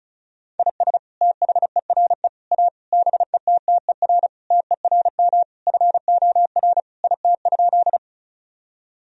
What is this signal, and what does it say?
Morse code, heard as a keyed tone, about 34 wpm, 700 Hz. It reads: IS THERE A BETTER TERM FOR IT?
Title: Sentences from Top 1000 Words - Review Part 4 - 35wpm